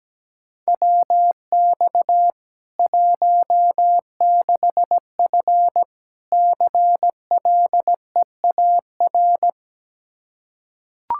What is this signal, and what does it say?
Morse code, heard as a keyed tone, about 17 wpm, 700 Hz. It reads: WX 16F CLEAR
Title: K1USN SST QSO Practice 17wpm